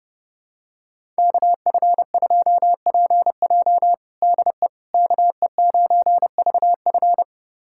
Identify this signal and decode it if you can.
Morse code, heard as a keyed tone, about 30 wpm, 700 Hz. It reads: KF2PJ DE KE9VF
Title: USA Callsign DE Callsign - 5 Characters - 30wpm